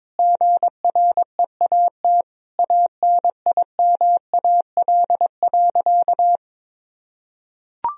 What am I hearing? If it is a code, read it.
Morse code, heard as a keyed tone, about 22 wpm, 700 Hz. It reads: GREAT ANIMAL.
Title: Sentences from Top 1000 Words - Review Part 1 - 22wpm